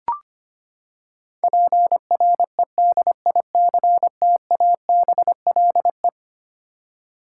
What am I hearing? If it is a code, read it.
Morse code, heard as a keyed tone, about 25 wpm, 700 Hz. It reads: PREDICTABLE
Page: morse-code-ninja-practice-25wpm